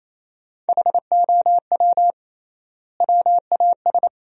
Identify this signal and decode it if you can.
Morse code, heard as a keyed tone, about 28 wpm, 700 Hz. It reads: HOW WAS
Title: 3x Word-Spacing - Sentences from Top 100 Words - 28wpm